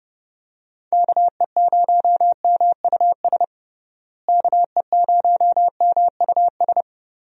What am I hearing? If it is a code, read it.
Morse code, heard as a keyed tone, about 29 wpm, 700 Hz. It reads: KE0MUS KE0MUS
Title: USA Call Signs - 6 Characters - Repeated 3x - 30wpm